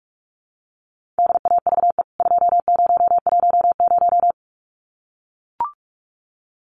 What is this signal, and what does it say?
Morse code, heard as a keyed tone, about 44 wpm, 700 Hz. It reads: DAVE 2010
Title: CWT Exchange 45wpm